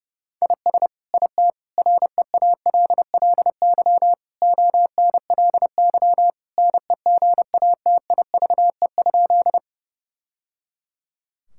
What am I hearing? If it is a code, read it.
Morse code, heard as a keyed tone, about 30 wpm, 700 Hz. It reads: IS IT REALLY ONLY NEGATIVE?
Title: Sentences from Top 1000 Words - Review Part 6 - 30wpm